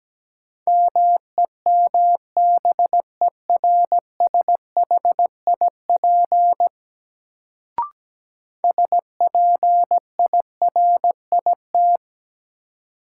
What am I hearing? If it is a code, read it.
Morse code, heard as a keyed tone, about 17 wpm, 700 Hz. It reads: MEMBERSHIP SPIRIT
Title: Sentences from Top 1400 Words - New Word Drill - 17wpm